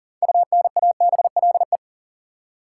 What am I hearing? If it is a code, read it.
Morse code, heard as a keyed tone, about 40 wpm, 700 Hz. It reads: UNABLE
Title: Six Letter Words - 40wpm